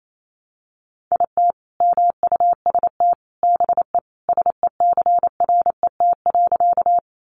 Morse code, heard as a keyed tone, about 28 wpm, 700 Hz. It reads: IT MUST BE SECRET.
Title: Sentences from Top 1200 Words 28wpm